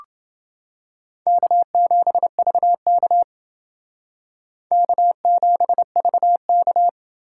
Morse code, heard as a keyed tone, about 29 wpm, 700 Hz. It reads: K7VK K7VK